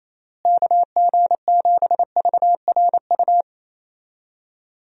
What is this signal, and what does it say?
Morse code, heard as a keyed tone, about 28 wpm, 700 Hz. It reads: KG7VRU
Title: USA Call Signs - 6 Characters - Repeated 3x - 28wpm